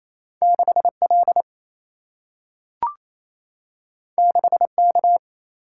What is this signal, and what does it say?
Morse code, heard as a keyed tone, about 28 wpm, 700 Hz. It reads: 6L 6K
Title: Single-Letter Single-Number 28wpm